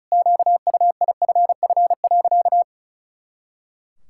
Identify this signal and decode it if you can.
Morse code, heard as a keyed tone, about 34 wpm, 700 Hz. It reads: QUIFF.
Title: Pangram Challenge v2 - 35wpm